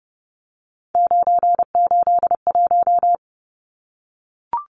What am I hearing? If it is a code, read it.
Morse code, heard as a keyed tone, about 30 wpm, 700 Hz. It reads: 981
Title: Numbers - 3-Digit 30wpm